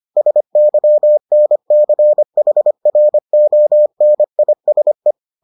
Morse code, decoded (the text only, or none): SYNCHRONISE